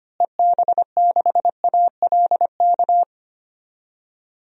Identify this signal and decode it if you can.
Morse code, heard as a keyed tone, about 25 wpm, 700 Hz. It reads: EB6ALK
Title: DXCC-100 Call Signs - 6 Characters - Repeated 3x - 25wpm